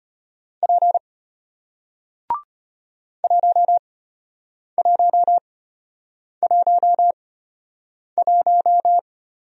Morse code, heard as a keyed tone, about 34 wpm, spaced as for 8 wpm, 700 Hz. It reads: P 1111